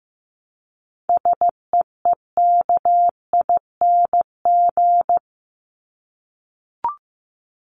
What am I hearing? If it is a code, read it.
Morse code, heard as a keyed tone, about 15 wpm, 700 Hz. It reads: SEEKING
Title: ING Suffix - Top 100 Words with Base Words 15wpm